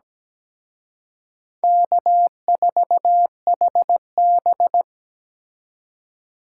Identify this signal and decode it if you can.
Morse code, heard as a keyed tone, about 20 wpm, 700 Hz. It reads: K4HB
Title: Speed Racing - USA Call Signs 17wpm